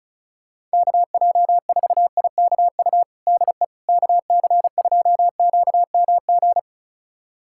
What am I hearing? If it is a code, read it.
Morse code, heard as a keyed tone, about 36 wpm, 700 Hz. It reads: KJ4IKU DE KC2QMG
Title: USA Callsign DE Callsign - 6 Characters - 35wpm